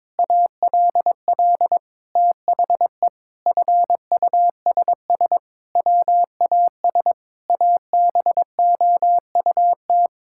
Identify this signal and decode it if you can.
Morse code, heard as a keyed tone, about 21 wpm, 700 Hz. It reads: ALL THE FUSS WAS ABOUT
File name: practice / 22wpm / Sets of 8 Words - 22wpm